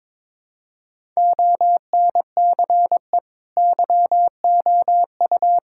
Morse code, heard as a keyed tone, about 22 wpm, 700 Hz. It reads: ONCE YOU
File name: practice / 22wpm / Sets of 7 Words - 22wpm